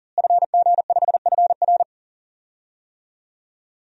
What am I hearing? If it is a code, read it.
Morse code, heard as a keyed tone, about 40 wpm, 700 Hz. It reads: FG5FR
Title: DXCC-100 Call Signs - 5 Characters - Repeated 2x - 40wpm